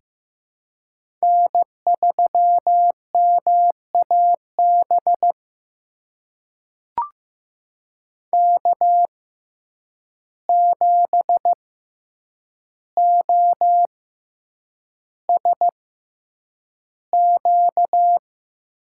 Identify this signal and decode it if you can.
Morse code, heard as a keyed tone, about 15 wpm, spaced as for 8 wpm, 700 Hz. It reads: N3MAB K7OSQ
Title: USA Call Signs - 5 Characters - 6x Character Spacing - 15wpm